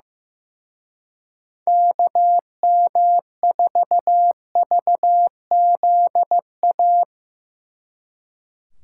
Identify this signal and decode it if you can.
Morse code, heard as a keyed tone, about 17 wpm, 700 Hz. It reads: KM4VZA
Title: Speed Racing - USA 2x3 Call Signs 15wpm